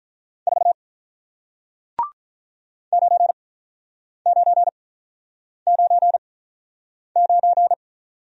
Morse code, heard as a keyed tone, about 48 wpm, 700 Hz. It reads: 4 9 9 9 9